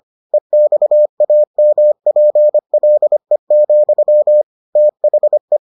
Morse code, heard as a keyed tone, about 25 wpm, 600 Hz. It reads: EXAMPLE, THE